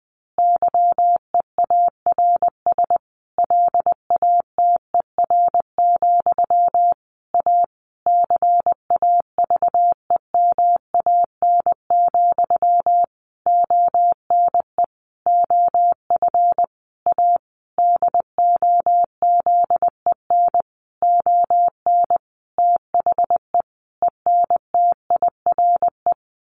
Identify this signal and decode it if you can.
Morse code, heard as a keyed tone, about 20 wpm, 700 Hz. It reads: YEARS LATER, A CAVEMAN, ONE OF A DOZEN ON THE ENTIRE
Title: A Sound Of Thunder 20wpm